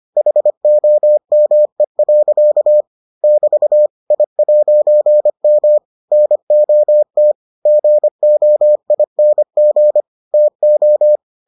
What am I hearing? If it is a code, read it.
Morse code, heard as a keyed tone, about 25 wpm, 600 Hz. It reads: HOME. <BT> I'M NOT GOING TO